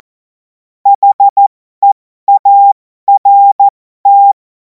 Morse code, heard as a keyed tone, about 14 wpm, spaced as for 12 wpm, 800 Hz. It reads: HEART
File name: Morsekurs / Witze